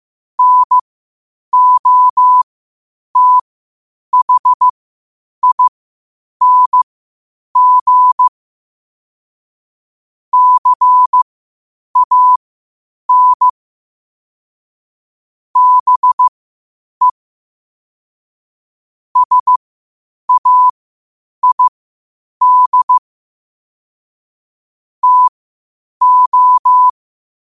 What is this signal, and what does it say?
Morse code, heard as a keyed tone, about 15 wpm, spaced as for 8 wpm, 1000 Hz. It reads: NOTHING CAN BE SAID TO
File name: qotdfiles / 8wpm